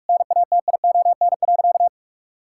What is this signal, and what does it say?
Morse code, heard as a keyed tone, about 46 wpm, 700 Hz. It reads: NATION.
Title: Sentences from Top 1000 Words - Review Part 6 - 45wpm